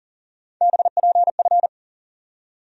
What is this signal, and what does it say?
Morse code, heard as a keyed tone, about 40 wpm, 700 Hz. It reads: BPF
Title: USA Call Sign Postfixes - 3 Letters 40wpm